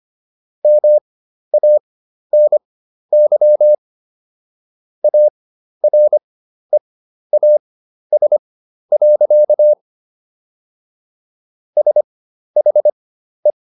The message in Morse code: MANY AREAS. SHE

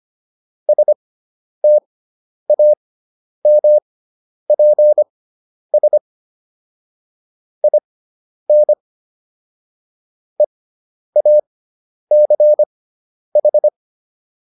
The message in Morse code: STAMPS IN EACH

